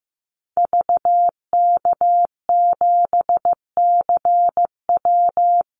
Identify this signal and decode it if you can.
Morse code, heard as a keyed tone, about 15 wpm, 700 Hz. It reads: VK7CW
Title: Call Signs Repeated 2x - 15wpm